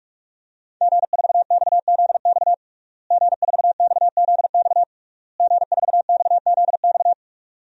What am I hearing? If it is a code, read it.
Morse code, heard as a keyed tone, about 46 wpm, 700 Hz. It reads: G4XZX G4XZX G4XZX
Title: DXCC-100 Call Signs - 5 Characters - Repeated 3x - 45wpm